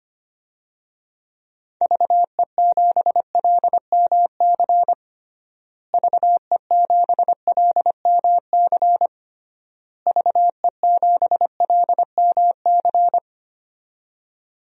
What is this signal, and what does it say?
Morse code, heard as a keyed tone, about 25 wpm, 700 Hz. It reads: VE7LMC VE7LMC VE7LMC
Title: DXCC-100 Call Signs - 6 Characters - Repeated 3x - 25wpm